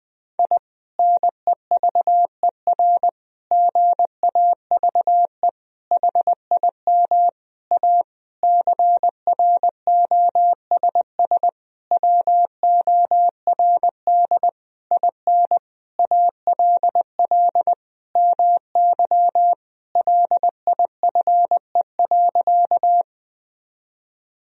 Morse code, heard as a keyed tone, about 20 wpm, 700 Hz. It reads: I NEVER GAVE HIM A CROSS WORD IN ALL MY LIFE.
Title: Sentences from Top 1000 Words - Review Part 1 - 20wpm